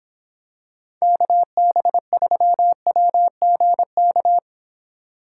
Morse code, heard as a keyed tone, about 26 wpm, 700 Hz. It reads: KB3WGK